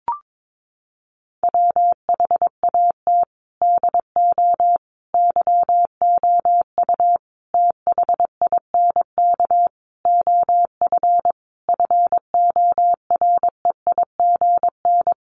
Morse code, heard as a keyed tone, about 22 wpm, 700 Hz. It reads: WHAT DO YOU THINK OF FOREIGN
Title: Sentences from Top 1400 Words 22wpm